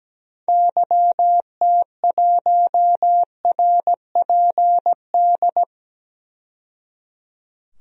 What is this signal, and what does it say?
Morse code, heard as a keyed tone, about 17 wpm, 700 Hz. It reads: YT1RPD
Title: DXCC-100 Call Signs - 6 Characters - 17wpm